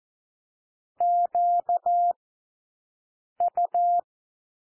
Morse code, heard as a keyed tone, about 14 wpm, 700 Hz. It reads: Q U
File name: cwmp3 / W1 SOUND WORDS 14 wpm 0